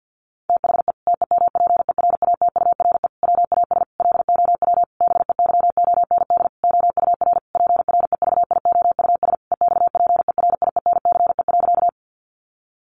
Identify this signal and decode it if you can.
Morse code, heard as a keyed tone, about 50 wpm, 700 Hz. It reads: THE TEMPERATURE WAS LOW BEYOND OUR PREVIOUS EXPERIENCE.
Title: Sentences from Top 1500 Words 50wpm